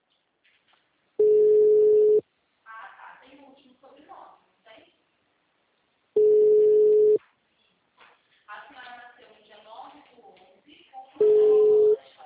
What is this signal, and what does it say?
Ringback tone (Europe, 425 Hz), with speech.
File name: out- -2002- - - .55775